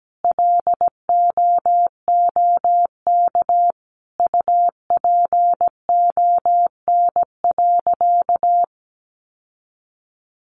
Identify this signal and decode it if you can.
Morse code, heard as a keyed tone, about 17 wpm, 700 Hz. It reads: LOOK UPON.
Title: Sentences from Top 1400 Words 17wpm